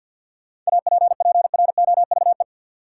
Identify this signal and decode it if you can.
Morse code, heard as a keyed tone, about 50 wpm, 700 Hz. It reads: APPROVE
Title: Seven Letter Words - 50wpm